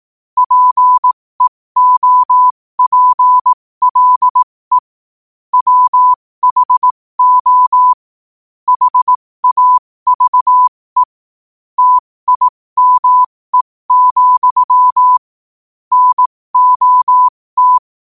Morse code, heard as a keyed tone, about 18 wpm, spaced as for 15 wpm, 1000 Hz. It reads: PEOPLE WHO HAVE TIME, NOT
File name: qotdfiles / 15wpm